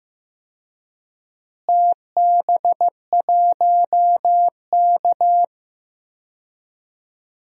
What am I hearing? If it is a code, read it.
Morse code, heard as a keyed tone, about 15 wpm, 700 Hz. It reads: TB1K
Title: DXCC-100 Call Signs - 4 Characters - Repeated 2x - 15wpm